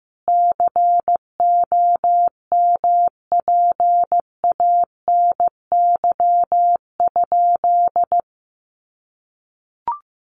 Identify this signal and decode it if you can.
Morse code, heard as a keyed tone, about 14 wpm, 700 Hz. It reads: COMPANY?
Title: Sentences from Top 1300 Words 15wpm